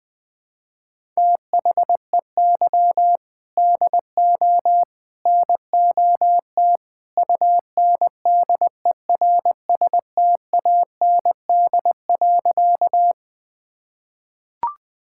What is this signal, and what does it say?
Morse code, heard as a keyed tone, about 20 wpm, 700 Hz. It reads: THEY DO NOT UNDERSTAND.
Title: Sentences with 3-Word Phrases - 20wpm